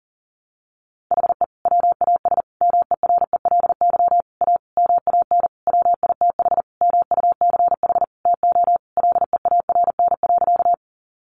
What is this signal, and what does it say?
Morse code, heard as a keyed tone, about 37 wpm, 700 Hz. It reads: HE WAS MERELY A MAN WITH MUCH TO LEARN.
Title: Sentences from Top 1400 Words 40wpm